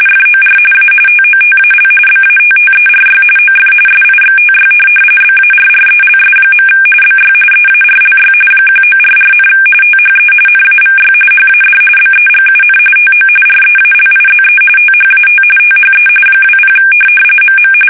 RTTY, about 45.45 baud. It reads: FOR A TOTAL BATTERY OF SIXTY BARRELS. BY EARLY 1944, THE NAVY HAD BEGUN TO REALIZE THAT THE 20 MM
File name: fsk850 / USS-Massachusetts